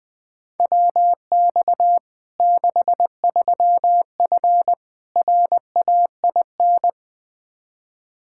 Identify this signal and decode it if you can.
Morse code, heard as a keyed tone, about 21 wpm, 700 Hz. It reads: WX 63F RAIN
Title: Practice QSO - Exchange 2 - Info - 20wpm